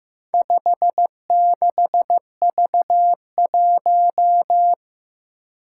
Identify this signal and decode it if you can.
Morse code, heard as a keyed tone, about 15 wpm, 700 Hz. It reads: 56V1
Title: Four Challenging Letter-Number Combinations 15wpm